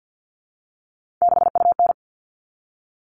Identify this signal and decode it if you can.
Morse code, heard as a keyed tone, about 50 wpm, 700 Hz. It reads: 6UN
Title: All Cut Numbers - 3-Digit 50wpm